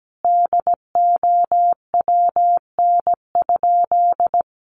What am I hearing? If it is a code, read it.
Morse code, heard as a keyed tone, about 17 wpm, 700 Hz. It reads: DOWN?
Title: Sentences from Top 1300 Words 17wpm